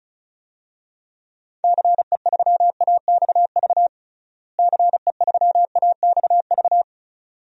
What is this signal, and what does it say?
Morse code, heard as a keyed tone, about 36 wpm, 700 Hz. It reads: CE3AXV CE3AXV
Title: DXCC-100 Call Signs - 6 Characters - Repeated 2x - 35wpm